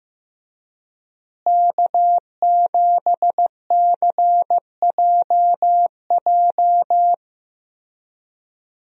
Morse code, heard as a keyed tone, about 16 wpm, 700 Hz. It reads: K7CJJ
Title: USA Call Signs - 5 Characters - Repeated 3x - 15wpm